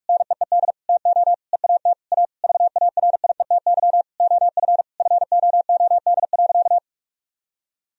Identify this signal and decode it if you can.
Morse code, heard as a keyed tone, about 42 wpm, 700 Hz. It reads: NEED TO EAT A VARIETY OF FOOD.
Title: Sentences from Top 1000 Words - Review Part 4 - 45wpm